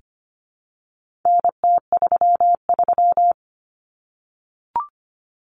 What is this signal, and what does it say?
Morse code, heard as a keyed tone, about 25 wpm, 700 Hz. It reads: NT33
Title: Common Cut Numbers - 4-Digit 25wpm